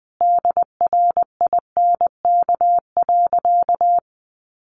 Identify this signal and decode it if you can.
Morse code, heard as a keyed tone, about 20 wpm, 700 Hz. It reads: DRINK.
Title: Sentences from Top 1500 Words 20wpm